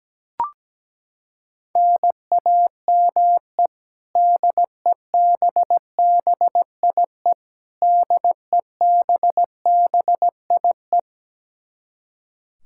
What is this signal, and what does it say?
Morse code, heard as a keyed tone, about 17 wpm, 700 Hz. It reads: NAME DEBBIE DEBBIE
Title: Practice QSO - Exchange 1 - Beginning - 17wpm